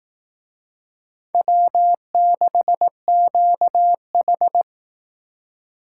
Morse code, heard as a keyed tone, about 18 wpm, 700 Hz. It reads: W6QH
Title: Speed Racing - USA 1x2 Call Signs 15wpm